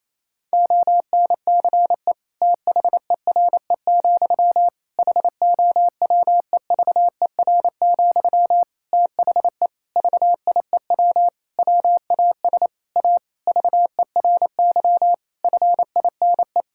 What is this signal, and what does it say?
Morse code, heard as a keyed tone, about 27 wpm, 700 Hz. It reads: ONCE THERE, HOWEVER, THE VIEW WAS A VERY FINE
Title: Sentences from Top 1000 Words - Review Part 6 - 28wpm